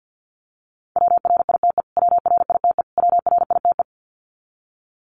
Morse code, heard as a keyed tone, about 50 wpm, 700 Hz. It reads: WRITE WRITE WRITE
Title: Top 200 Words Repeated 50wpm